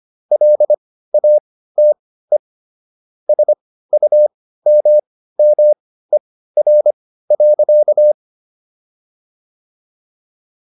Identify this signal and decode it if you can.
Morse code, heard as a keyed tone, about 25 wpm, spaced as for 15 wpm, 600 Hz. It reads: LATE SUMMER.